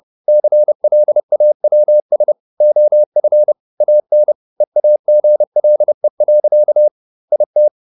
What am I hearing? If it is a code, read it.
Morse code, heard as a keyed tone, about 30 wpm, 600 Hz. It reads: CLAWS OF AN EAGLE. IT
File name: the old man and the sea / the old man and the sea 1 30wpm